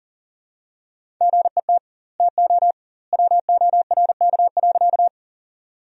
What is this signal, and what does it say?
Morse code, heard as a keyed tone, about 40 wpm, 700 Hz. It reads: GET TO WORK.
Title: 2x Word-Spacing - Sentences from Top 200 Words - 40wpm